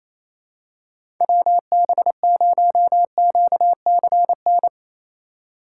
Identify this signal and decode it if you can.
Morse code, heard as a keyed tone, about 28 wpm, 700 Hz. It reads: WB0QCN
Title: USA Call Signs - 6 Characters - 2x Character Spacing - 28wpm